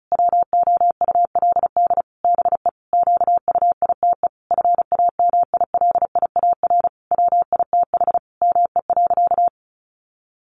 Morse code, heard as a keyed tone, about 34 wpm, 700 Hz. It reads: WOULD BE QUITE FAMILIAR WITH ME.
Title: Sentences from Top 1000 Words 35wpm